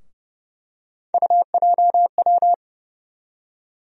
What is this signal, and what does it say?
Morse code, heard as a keyed tone, about 30 wpm, 700 Hz. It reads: UJW
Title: USA Call Sign Postfixes - 3 Letters 30wpm